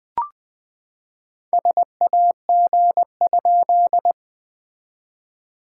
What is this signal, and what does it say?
Morse code, heard as a keyed tone, about 19 wpm, 700 Hz. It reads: SAG?
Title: SOTA QSO Practice 20wpm